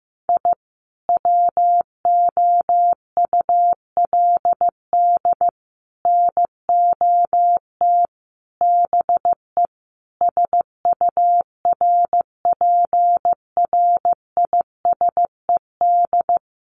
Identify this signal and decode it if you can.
Morse code, heard as a keyed tone, about 15 wpm, 700 Hz. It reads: I WOULD NOT BE SURPRISED
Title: Sets of 5 Words Encore2 15wpm